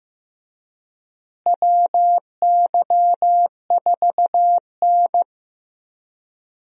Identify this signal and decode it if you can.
Morse code, heard as a keyed tone, about 15 wpm, 700 Hz. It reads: WY4N
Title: USA Call Signs - 4 Characters - Repeated 2x - 15wpm